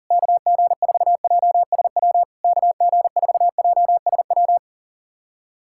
Morse code, heard as a keyed tone, about 40 wpm, 690 Hz. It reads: KG4JSW KG4JSW
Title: SOTA QSO Practice 40wpm